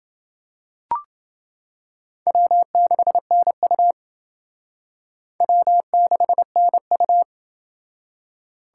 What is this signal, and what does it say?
Morse code, heard as a keyed tone, about 28 wpm, 700 Hz. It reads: W6NU W6NU